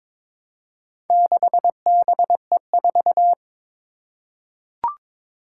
Morse code, heard as a keyed tone, about 22 wpm, 700 Hz. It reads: 6BE4
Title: Four Challenging Letter-Number Combinations 22wpm